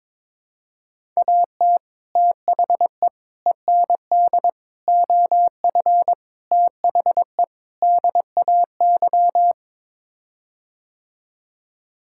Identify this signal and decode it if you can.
Morse code, heard as a keyed tone, about 22 wpm, 700 Hz. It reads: AT THE END OF THE DAY
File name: practice / 22wpm / Sets of 6 Words - 22wpm